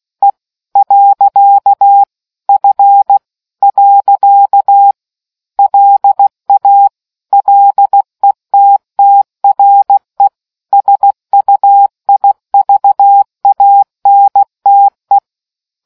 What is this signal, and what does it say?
Morse code, heard as a keyed tone, about 16 wpm, 800 Hz. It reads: E . F . LA LETTRE SUIVANTE